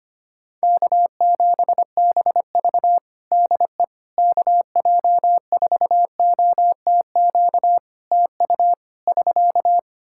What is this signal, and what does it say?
Morse code, heard as a keyed tone, about 25 wpm, 700 Hz. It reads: K7BV DE KJ4OTQ TU <SK>
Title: Practice QSO - Exchange 4 - Ending - 25wpm